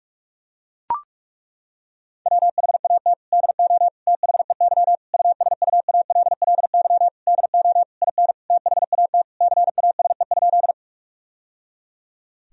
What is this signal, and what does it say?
Morse code, heard as a keyed tone, about 41 wpm, 700 Hz. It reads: WHAT DO THEY USUALLY DO IN THAT CASE?
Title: Sentences from Top 1000 Words - Review Part 3 - 45wpm